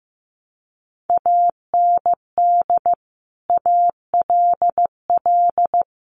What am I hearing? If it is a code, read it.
Morse code, heard as a keyed tone, about 15 wpm, 700 Hz. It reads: AND ALL
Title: Sentences from Top 100 Words 15wpm